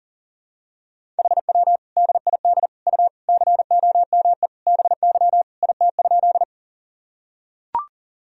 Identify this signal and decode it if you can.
Morse code, heard as a keyed tone, about 40 wpm, 700 Hz. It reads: HW DID U COME BY IT?
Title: Sentences from Top 100 Words and Abbreviations 40wpm